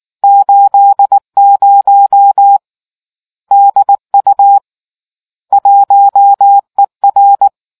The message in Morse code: 80 DU 1ER

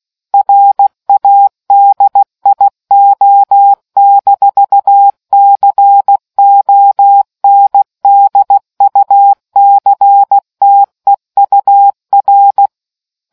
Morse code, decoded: RADIO-CONDUCTEUR

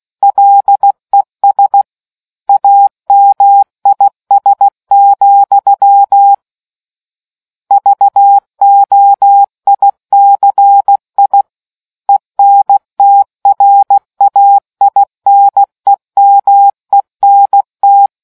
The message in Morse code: LES AMIS, VOICI ENTRAINEMENT

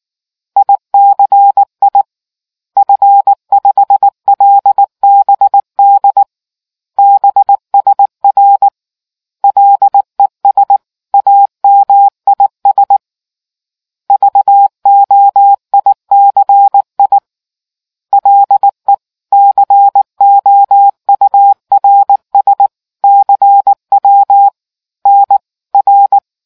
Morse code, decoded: ICI F5LBD BSR LES AMIS VOICI LE COURS CW N R

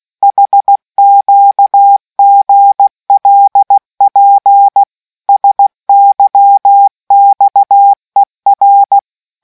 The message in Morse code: HQGLP SYXER

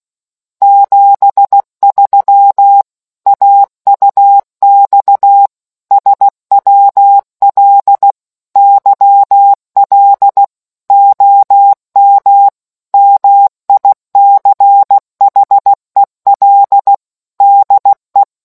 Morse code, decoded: 73 AUX SWL YL OM MICHEL DE

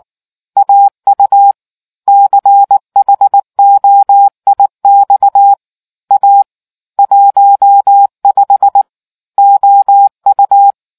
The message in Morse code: AU CHOIX A 15 OU

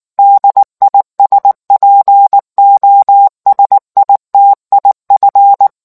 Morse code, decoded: DISPOSITIF

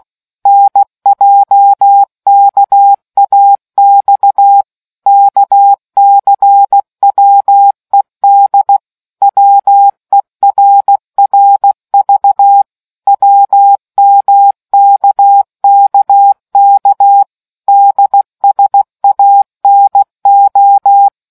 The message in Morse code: NJKAX KCWED WERRV WMKKK DSANO

